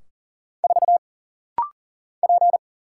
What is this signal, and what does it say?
Morse code, heard as a keyed tone, about 40 wpm, 700 Hz. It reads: 4 P
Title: Single Letter-Number - Rapid-Fire 40wpm